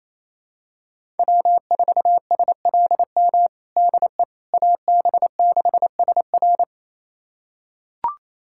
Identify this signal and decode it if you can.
Morse code, heard as a keyed tone, about 28 wpm, 700 Hz. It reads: W4SLM DE AB6SR
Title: USA Callsign DE Callsign - 5 Characters - 28wpm